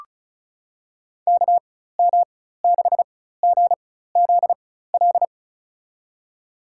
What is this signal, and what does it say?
Morse code, heard as a keyed tone, about 34 wpm, spaced as for 16 wpm, 700 Hz. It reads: KM6GZL
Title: USA Call Signs - 6 Characters - 4x Character Spacing - 35wpm